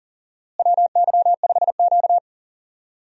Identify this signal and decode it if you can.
Morse code, heard as a keyed tone, about 40 wpm, 700 Hz. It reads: WY5Q
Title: USA Call Signs - 4 Characters - Repeated 2x - 40wpm